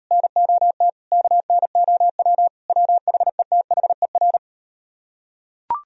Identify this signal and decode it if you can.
Morse code, heard as a keyed tone, about 38 wpm, 700 Hz. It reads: NOT KNOW WHETHER